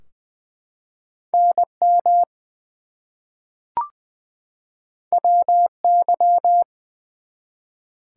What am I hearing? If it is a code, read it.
Morse code, heard as a keyed tone, about 20 wpm, 700 Hz. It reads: NM WY
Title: US States and Canadian Provinces 20wpm